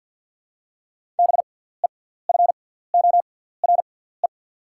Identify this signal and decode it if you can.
Morse code, heard as a keyed tone, about 48 wpm, spaced as for 17 wpm, 700 Hz. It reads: BEFORE
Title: 6x Character Spacing - Top 300 Words - 50wpm